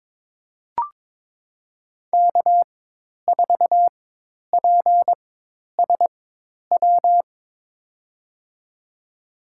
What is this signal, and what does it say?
Morse code, heard as a keyed tone, about 22 wpm, spaced as for 10 wpm, 700 Hz. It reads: K4PSW
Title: USA Call Signs - 5 Characters - 4x Character Spacing - 22wpm